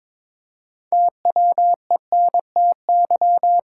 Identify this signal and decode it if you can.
Morse code, heard as a keyed tone, about 22 wpm, 700 Hz. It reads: TWENTY
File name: practice / 22wpm / Sentences with 2-Word Phrases - 22wpm